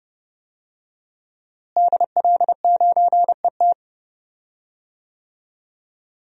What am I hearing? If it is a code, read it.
Morse code, heard as a keyed tone, about 32 wpm, 700 Hz. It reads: DL9ET
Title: DXCC-100 Call Signs - 5 Characters - Repeated 3x - 30wpm